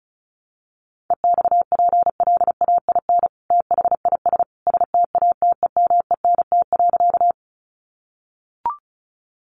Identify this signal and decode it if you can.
Morse code, heard as a keyed tone, about 34 wpm, 700 Hz. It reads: EXPLAIN THIS STATEMENT.
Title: Sentences from Top 1200 Words 35wpm